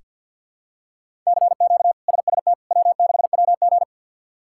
Morse code, heard as a keyed tone, about 46 wpm, 700 Hz. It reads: CQ SST W6WG
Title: K1USN SST QSO Practice - 50wpm